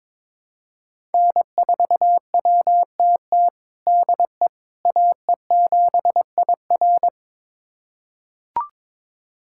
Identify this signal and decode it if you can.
Morse code, heard as a keyed tone, about 22 wpm, 700 Hz. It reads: N4WTT DE AE7IR
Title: USA Callsign DE Callsign - 5 Characters - 22wpm